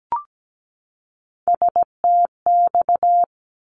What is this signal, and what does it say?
Morse code, heard as a keyed tone, about 17 wpm, 700 Hz. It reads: STX
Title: ARRL Sections 17wpm